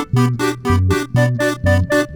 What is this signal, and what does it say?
There is no code.